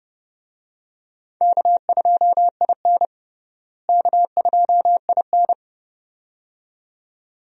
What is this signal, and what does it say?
Morse code, heard as a keyed tone, about 30 wpm, 700 Hz. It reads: K2IN K2IN
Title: USA Call Signs - 4 Characters - Repeated 2x - 30wpm